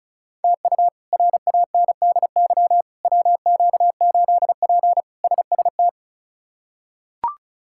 Morse code, heard as a keyed tone, about 34 wpm, 700 Hz. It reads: TU RANDY WQ8P SST
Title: K1USN SST QSO Practice - 35wpm